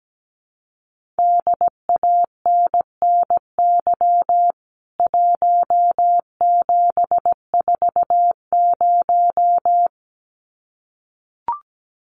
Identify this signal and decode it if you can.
Morse code, heard as a keyed tone, about 17 wpm, 700 Hz. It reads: DANNY 1740
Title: CWT Exchange 17wpm